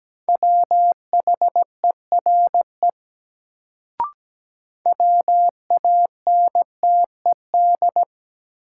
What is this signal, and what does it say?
Morse code, heard as a keyed tone, about 17 wpm, 700 Hz. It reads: WHERE WANTED
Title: Sets of 2 Words - New Word Drill - 17wpm